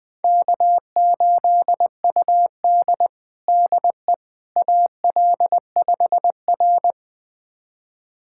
Morse code, heard as a keyed tone, about 21 wpm, 700 Hz. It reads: K8UD DE AL5R
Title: USA Callsign DE Callsign - 4 Characters - 20wpm